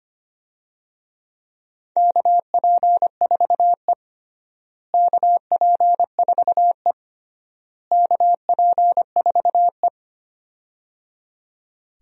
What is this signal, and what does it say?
Morse code, heard as a keyed tone, about 25 wpm, 700 Hz. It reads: KP4E KP4E KP4E
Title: DXCC-100 Call Signs - 4 Characters - Repeated 3x - 25wpm